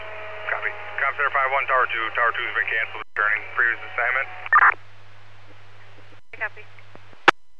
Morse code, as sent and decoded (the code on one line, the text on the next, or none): none